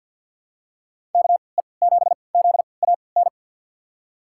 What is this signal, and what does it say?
Morse code, heard as a keyed tone, about 52 wpm, spaced as for 28 wpm, 700 Hz. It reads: KE7ZAN
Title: USA Call Signs - 3x Character Spacing - 50wpm